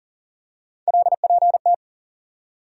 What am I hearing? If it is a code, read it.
Morse code, heard as a keyed tone, about 40 wpm, 700 Hz. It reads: LPT
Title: USA Call Sign Postfixes - 3 Letters 40wpm